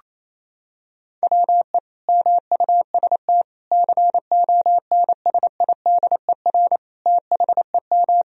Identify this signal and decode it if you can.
Morse code, heard as a keyed tone, about 28 wpm, 700 Hz. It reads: WE MUST CONSIDER THEM
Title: Sentences from Top 1000 Words - Review Part 4 - 28wpm